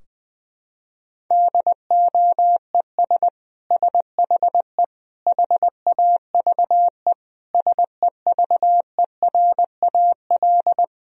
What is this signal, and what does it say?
Morse code, heard as a keyed tone, about 20 wpm, 700 Hz. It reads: DOES SHE HAVE SEVERAL
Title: Sentences from Top 1000 Words - Review Part 2 - 20wpm